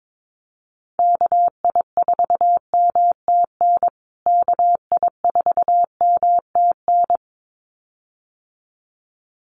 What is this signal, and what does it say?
Morse code, heard as a keyed tone, about 22 wpm, 700 Hz. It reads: KI4MTN KI4MTN
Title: USA Call Signs Repeated 2x 22wpm